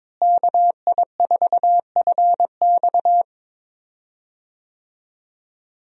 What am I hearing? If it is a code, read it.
Morse code, heard as a keyed tone, about 21 wpm, 700 Hz. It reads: KI4FX
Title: USA Call Signs - 5 Characters - Repeated 3x - 22wpm